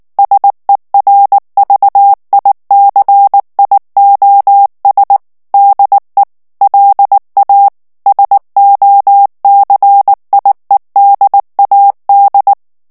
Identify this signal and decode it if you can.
Morse code, heard as a keyed tone, about 19 wpm, 800 Hz. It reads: SERVICIOS DE LA SOCIEDAD
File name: morse mp3 / ESPAGNOL TEXTE EN CLAIR 2 VITESSE 18 MOTS MINUTES